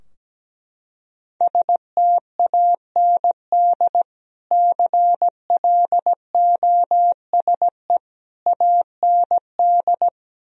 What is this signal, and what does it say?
Morse code, heard as a keyed tone, about 17 wpm, 700 Hz. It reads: STAND CLOSE AND
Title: Sentences from Top 1000 Words - Review Part 2 - 17wpm